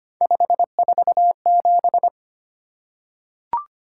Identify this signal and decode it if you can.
Morse code, heard as a keyed tone, about 25 wpm, 700 Hz. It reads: 547
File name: Three Letter-Number Combinations - 25wpm